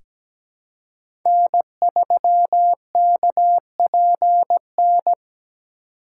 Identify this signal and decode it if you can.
Morse code, heard as a keyed tone, about 17 wpm, 700 Hz. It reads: N3KPN
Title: USA Call Signs - 5 Characters - 2x Character Spacing - 17wpm